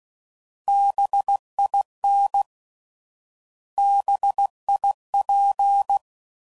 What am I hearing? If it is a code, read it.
Morse code, heard as a keyed tone, about 16 wpm, 800 Hz. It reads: BIN BIP